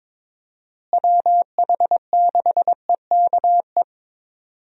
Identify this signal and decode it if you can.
Morse code, heard as a keyed tone, about 22 wpm, 700 Hz. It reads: WH6EKE
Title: USA Call Signs - 6 Characters - Repeated 2x - 22wpm